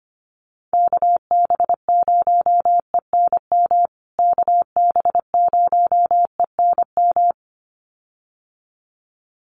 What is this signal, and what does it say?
Morse code, heard as a keyed tone, about 25 wpm, 700 Hz. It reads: KB0ENM KB0ENM
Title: USA Call Signs Repeated 2x 25wpm